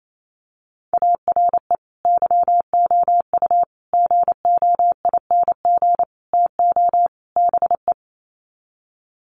Morse code, heard as a keyed tone, about 28 wpm, 700 Hz. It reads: ARE YOU GOING TO BE
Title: Sets of 5 Words Encore 28wpm